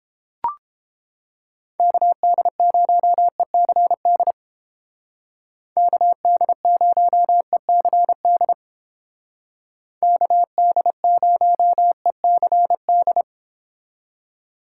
Morse code, heard as a keyed tone, about 31 wpm, 700 Hz. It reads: KD0ECD KD0ECD KD0ECD